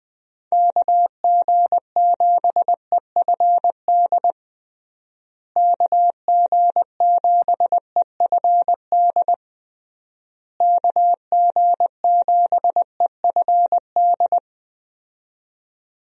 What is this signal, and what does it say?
Morse code, heard as a keyed tone, about 21 wpm, 700 Hz. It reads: KG7EFD KG7EFD KG7EFD
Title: USA Call Signs - 6 Characters - Repeated 3x - 20wpm